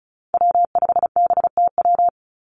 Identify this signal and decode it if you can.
Morse code, heard as a keyed tone, about 36 wpm, 700 Hz. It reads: W5BTW
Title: USA Call Signs 35wpm